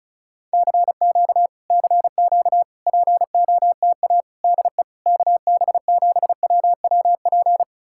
Morse code, heard as a keyed tone, about 40 wpm, 700 Hz. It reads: CQ CQ POTA DE KB7WWP